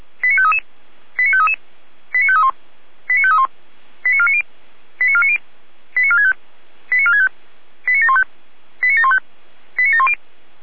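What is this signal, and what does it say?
Five-tone selective call (ZVEI), heard as quick runs of five tones, tones of about 70 ms. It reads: 89533 89533 89531 89531 89490 89490 89465 89465 89715 89715 89710
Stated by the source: ZVEI five-tone call received on SDR.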